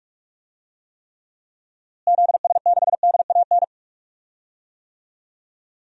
Morse code, heard as a keyed tone, about 46 wpm, 700 Hz. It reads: ZS6DAN
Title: DXCC-100 Call Signs - 6 Characters - Repeated 3x - 45wpm